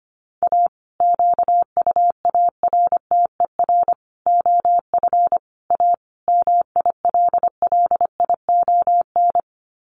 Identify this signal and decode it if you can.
Morse code, heard as a keyed tone, about 25 wpm, 700 Hz. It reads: A QUARTER OF A MILLION
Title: Sets of 5 Words Encore2 25wpm